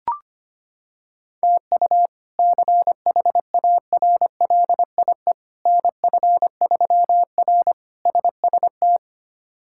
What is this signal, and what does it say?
Morse code, heard as a keyed tone, about 25 wpm, 700 Hz. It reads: TU CHARLIE NF3R SST
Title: K1USN SST QSO Practice - 25wpm